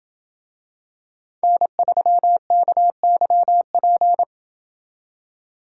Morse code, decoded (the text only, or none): N3KYP